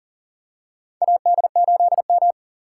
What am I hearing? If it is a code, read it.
Morse code, heard as a keyed tone, about 40 wpm, 700 Hz. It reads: AD8M
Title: USA Call Signs - 4 Characters - 4x Character Spacing - 40wpm